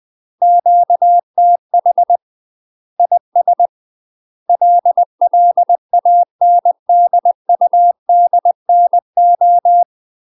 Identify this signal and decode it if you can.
Morse code, heard as a keyed tone, about 20 wpm, 700 Hz. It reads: QTH IS LLANDUDNO
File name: UK MASTER QSOs TRAINING / 20WPM / UK MASTER QSOs 0068